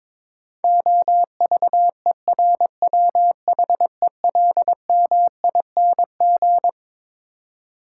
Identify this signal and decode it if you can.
Morse code, heard as a keyed tone, about 22 wpm, 700 Hz. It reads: OVERWHELMING
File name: Twelve Letter Words - 22wpm